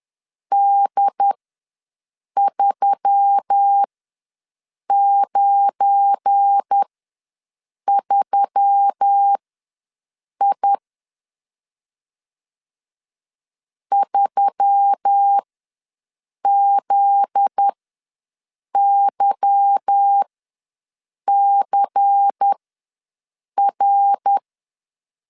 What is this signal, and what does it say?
Morse code, heard as a keyed tone, about 10 wpm, spaced as for 6 wpm, 800 Hz. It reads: D393I 3ZYCR